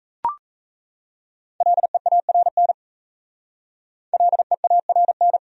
Morse code, heard as a keyed tone, about 36 wpm, 700 Hz. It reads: LEARN LEARN